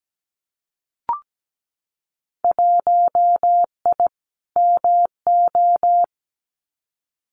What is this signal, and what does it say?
Morse code, heard as a keyed tone, about 17 wpm, 700 Hz. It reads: 1I MO
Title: Winter Field Day Exchanges Encore 17wpm